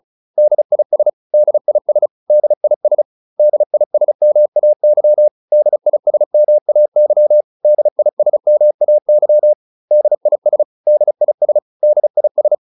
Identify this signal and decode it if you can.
Morse code, heard as a keyed tone, about 34 wpm, 600 Hz. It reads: DIS DIS DIS DISMAY DISMAY DISMAY DIS DIS DIS
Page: advanced-practice-files